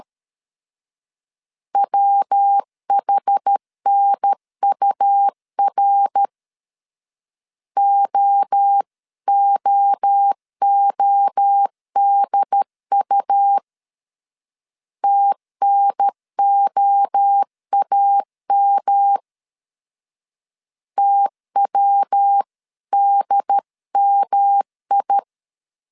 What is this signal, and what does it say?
Morse code, heard as a keyed tone, about 13 wpm, 800 Hz. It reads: WHNUR OOODU TNOAM TWDMI
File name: CW / DET / 1 3 1500C S B 12wpm 2828